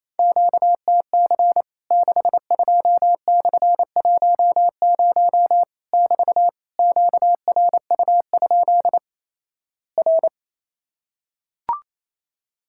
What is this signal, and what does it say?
Morse code, heard as a keyed tone, about 28 wpm, 700 Hz. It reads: QTC 62/10 <BT> QRU? R
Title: Worked All Europe DX Contest - QTC Exchange - 28wpm